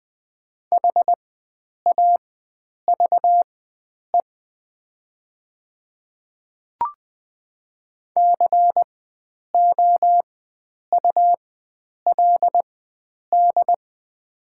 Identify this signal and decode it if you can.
Morse code, heard as a keyed tone, about 20 wpm, spaced as for 9 wpm, 700 Hz. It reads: HAVE COULD
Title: 4x Character Spacing - Top 100 Words - 20wpm